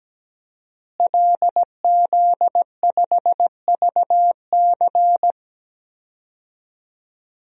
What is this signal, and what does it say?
Morse code, heard as a keyed tone, about 17 wpm, 700 Hz. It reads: LZ5VC
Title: DXCC-100 Call Signs - 5 Characters - Repeated 3x - 17wpm